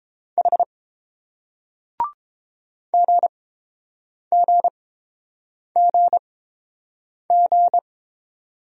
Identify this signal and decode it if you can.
Morse code, heard as a keyed tone, about 30 wpm, spaced as for 7 wpm, 700 Hz. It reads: H GGGG